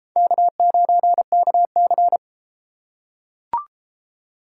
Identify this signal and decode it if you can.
Morse code, heard as a keyed tone, about 31 wpm, 700 Hz. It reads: K9KC